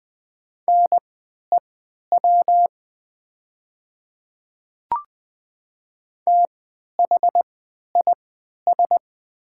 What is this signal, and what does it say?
Morse code, heard as a keyed tone, about 20 wpm, spaced as for 11 wpm, 700 Hz. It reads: NEW THIS
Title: 3x Character Spacing - Top 200 Words - 20wpm